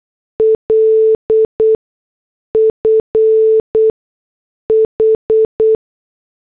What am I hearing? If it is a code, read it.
Morse code, heard as a keyed tone, about 8 wpm, spaced as for 6 wpm, 440 Hz. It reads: LF H
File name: morse / headlines